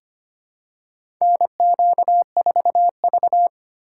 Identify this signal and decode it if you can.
Morse code, heard as a keyed tone, about 25 wpm, 700 Hz. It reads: NQ4V
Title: USA Call Signs - 3x Character Spacing - 25wpm